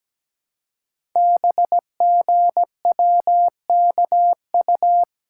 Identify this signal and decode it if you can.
Morse code, heard as a keyed tone, about 17 wpm, 700 Hz. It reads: BGWKU
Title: Code Groups - 5 Letters - 17wpm